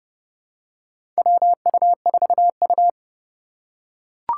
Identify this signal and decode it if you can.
Morse code, heard as a keyed tone, about 32 wpm, 700 Hz. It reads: WU4U
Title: USA Call Signs - 4 Characters - 3x Character Spacing - 30wpm